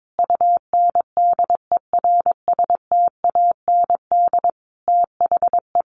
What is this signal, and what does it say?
Morse code, heard as a keyed tone, about 22 wpm, 700 Hz. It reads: UNDERSTAND THE
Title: Sentences from Top 600 Words 22wpm